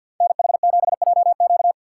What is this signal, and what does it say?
Morse code, heard as a keyed tone, about 48 wpm, 700 Hz. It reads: NH7JQ
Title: USA Call Signs - 5 Characters - 4x Character Spacing - 50wpm